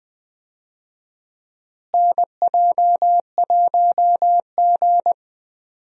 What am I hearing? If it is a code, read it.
Morse code, heard as a keyed tone, about 20 wpm, 700 Hz. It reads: NJ1G
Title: USA Call Signs - 4 Characters - Repeated 2x - 20wpm